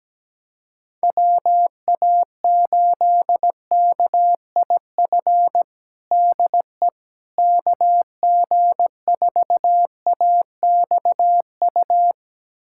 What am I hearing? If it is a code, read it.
Morse code, heard as a keyed tone, about 17 wpm, 700 Hz. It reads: WA8KIF DE KG4AXU
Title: USA Callsign DE Callsign - 6 Characters - 17wpm